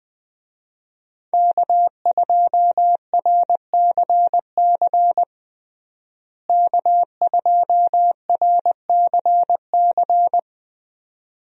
Morse code, heard as a keyed tone, about 20 wpm, 700 Hz. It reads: K2RCC K2RCC
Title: USA Call Signs - 5 Characters - Repeated 2x - 20wpm